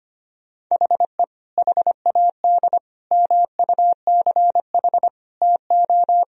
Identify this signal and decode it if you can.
Morse code, heard as a keyed tone, about 25 wpm, 700 Hz. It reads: HE HAD MUCH TO
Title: Sentences from Top 1000 Words - Review Part 4 - 25wpm